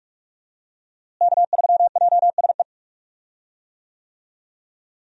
Morse code, heard as a keyed tone, about 42 wpm, 700 Hz. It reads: K3JSE
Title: USA Call Signs - 5 Characters - Repeated 3x - 45wpm